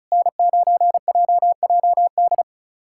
Morse code, heard as a keyed tone, about 34 wpm, 700 Hz. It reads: N9JJD
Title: USA Callsign DE Callsign 35wpm